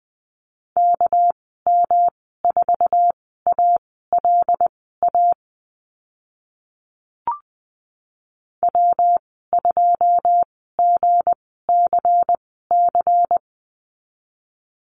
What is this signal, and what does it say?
Morse code, heard as a keyed tone, about 20 wpm, spaced as for 14 wpm, 700 Hz. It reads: KM4ALA W2GCC
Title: 207 Lesson 33g - Letter j - Call Signs 20wpm